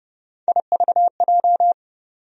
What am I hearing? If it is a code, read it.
Morse code, heard as a keyed tone, about 30 wpm, 700 Hz. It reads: IVJ
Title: USA Call Sign Postfixes - 3 Letters 30wpm